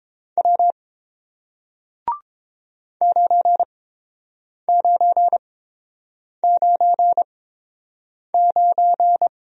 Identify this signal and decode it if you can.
Morse code, heard as a keyed tone, about 30 wpm, spaced as for 7 wpm, 700 Hz. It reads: W 9999